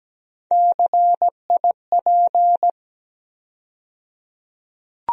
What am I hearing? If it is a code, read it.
Morse code, heard as a keyed tone, about 17 wpm, 700 Hz. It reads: CIP
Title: USA Call Sign Postfixes - 3 Letters 17wpm